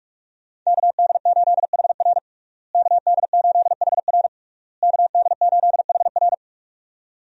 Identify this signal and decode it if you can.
Morse code, heard as a keyed tone, about 41 wpm, 700 Hz. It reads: KD8HR KD8HR KD8HR
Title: USA Call Signs - 5 Characters - Repeated 3x - 45wpm